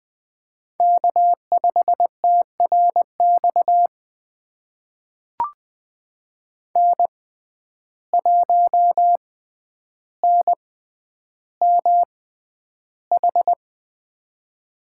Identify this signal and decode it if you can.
Morse code, heard as a keyed tone, about 20 wpm, spaced as for 10 wpm, 700 Hz. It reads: K5TRX N1NMH
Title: USA Call Signs - 5 Characters - 6x Character Spacing - 20wpm